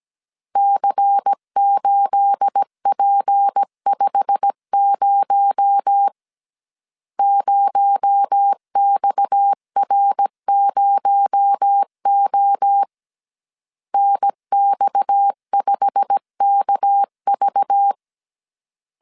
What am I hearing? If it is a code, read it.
Morse code, heard as a keyed tone, about 17 wpm, 800 Hz. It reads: C8P50 0XR0O NX5KV
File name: CW / DET / 1 9 1500C S A 16wpm 2828